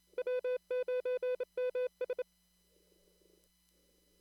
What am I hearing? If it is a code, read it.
Morse code, heard as a keyed tone, about 28 wpm, 500 Hz. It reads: W9MS
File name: CW / kt5j wpx cw-op-k5pi